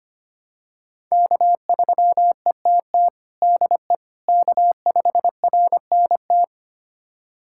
Morse code, heard as a keyed tone, about 25 wpm, 700 Hz. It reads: K3ETT DE K5RNT
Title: USA Callsign DE Callsign - 5 Characters - 25wpm